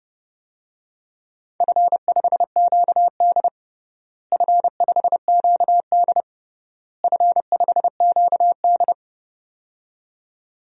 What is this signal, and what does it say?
Morse code, heard as a keyed tone, about 32 wpm, 700 Hz. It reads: F5QD F5QD F5QD
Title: DXCC-100 Call Signs - 4 Characters - Repeated 3x - 30wpm